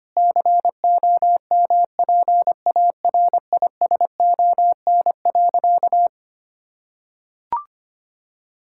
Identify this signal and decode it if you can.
Morse code, heard as a keyed tone, about 25 wpm, 700 Hz. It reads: COMPARISON.
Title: Sentences from Top 1600 Words 25wpm